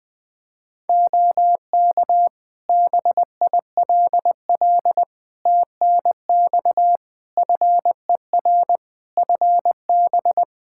Morse code, heard as a keyed tone, about 20 wpm, 700 Hz. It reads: OK BILL TNX FER FB
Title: Practice QSO - Exchange 4 - Ending - 20wpm